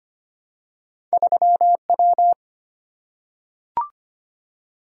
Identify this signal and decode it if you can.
Morse code, heard as a keyed tone, about 25 wpm, 700 Hz. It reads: 3W
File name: Two Letter-Number Combinations - 25wpm